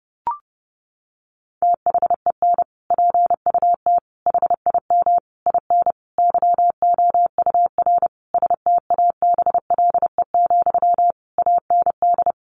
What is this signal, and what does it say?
Morse code, heard as a keyed tone, about 30 wpm, 700 Hz. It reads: THEN PUT HIM IN YOUR STABLE, AND
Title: Sentences from Top 1500 Words 30wpm